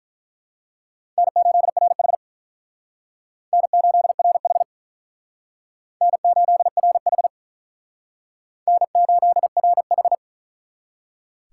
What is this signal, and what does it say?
Morse code, heard as a keyed tone, about 42 wpm, 700 Hz. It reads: N8RH N8RH N8RH N8RH
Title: Speed Racing - USA Call Signs 35wpm